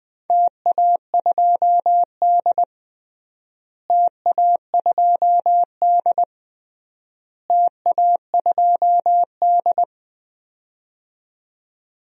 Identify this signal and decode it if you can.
Morse code, heard as a keyed tone, about 21 wpm, 700 Hz. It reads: TA2D TA2D TA2D
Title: DXCC-100 Call Signs - 4 Characters - Repeated 3x - 20wpm